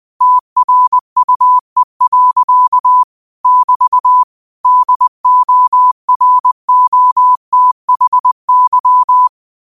Morse code, heard as a keyed tone, about 20 wpm, 1000 Hz. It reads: TRUE. <BT> DOROTHY